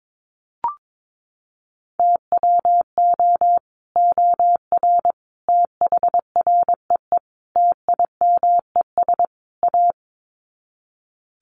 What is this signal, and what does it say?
Morse code, heard as a keyed tone, about 22 wpm, 700 Hz. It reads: TWO OR THREE TIMES A
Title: Sets of 5 Words Encore2 22wpm